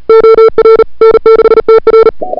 Morse code, heard as a keyed tone, about 37 wpm, 440 Hz. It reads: OR N6TR
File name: n6tr / Fab2007CWSprintHour1